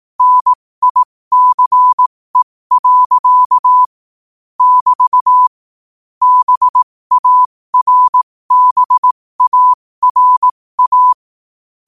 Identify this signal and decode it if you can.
Morse code, heard as a keyed tone, about 18 wpm, spaced as for 15 wpm, 1000 Hz. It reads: NICE. <BT> BARBARA
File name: qotdfiles / 15wpm